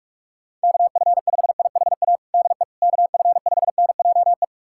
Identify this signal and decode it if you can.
Morse code, heard as a keyed tone, about 47 wpm, 700 Hz. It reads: KF5IHA DE KF5NJE
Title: USA Callsign DE Callsign 45wpm